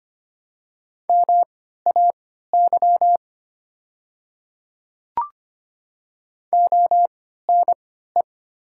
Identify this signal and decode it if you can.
Morse code, heard as a keyed tone, about 25 wpm, spaced as for 14 wpm, 700 Hz. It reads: MAY ONE
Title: 3x Character Spacing - Top 100 Words - 25wpm